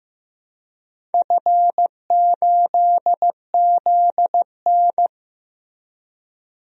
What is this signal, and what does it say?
Morse code, heard as a keyed tone, about 15 wpm, 700 Hz. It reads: F8ZN
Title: DXCC-100 Call Signs - 4 Characters - Repeated 3x - 15wpm